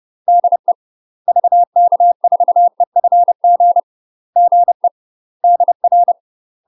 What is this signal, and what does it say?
Morse code, heard as a keyed tone, about 30 wpm, 700 Hz. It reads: DE VK4EFG GE DR
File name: UK MASTER QSOs TRAINING / 30WPM / UK MASTER QSOs 0020